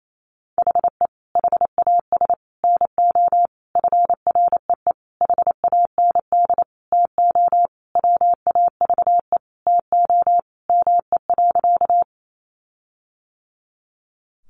Morse code, heard as a keyed tone, about 28 wpm, 700 Hz. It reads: HE HAS NO FREE HAND TO WAVE TO ME.
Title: Sentences from Top 1400 Words 28wpm